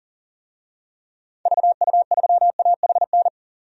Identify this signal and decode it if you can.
Morse code, heard as a keyed tone, about 40 wpm, 700 Hz. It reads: VU3AHN
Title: DXCC-100 Call Signs - 6 Characters - 40wpm